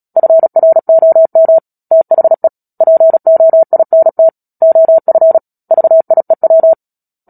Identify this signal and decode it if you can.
Morse code, heard as a keyed tone, about 37 wpm, 650 Hz. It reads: FROM THE POINT OF VIEW
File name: Book Two / 10-Chapter 10 - Epilogue - 36 WPM